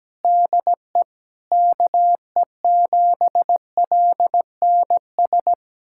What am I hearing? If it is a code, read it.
Morse code, heard as a keyed tone, about 17 wpm, 700 Hz. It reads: DE KE7LNS
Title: USA Callsign DE Callsign - 6 Characters - 17wpm